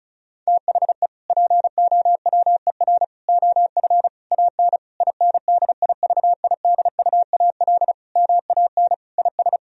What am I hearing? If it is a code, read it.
Morse code, heard as a keyed tone, about 34 wpm, 700 Hz. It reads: THE POWER OF AN INDIVIDUAL MAN IS
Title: Sentences from Top 1000 Words - Review Part 4 - 35wpm